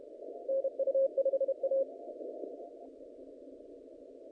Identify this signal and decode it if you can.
Morse code, heard as a keyed tone, about 32 wpm, 540 Hz. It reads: NU5A
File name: contests / naqp-cw-k5tr-op-k5pi- -0300